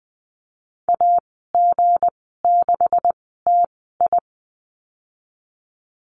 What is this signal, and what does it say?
Morse code, heard as a keyed tone, about 20 wpm, spaced as for 15 wpm, 700 Hz. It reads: AG6TI
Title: 173 Lesson 28g - Number 9 - Call Signs 20wpm